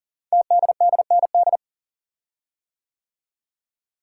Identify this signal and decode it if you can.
Morse code, heard as a keyed tone, about 41 wpm, 700 Hz. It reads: TDDND
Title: Code Groups - 5 Letters - 40wpm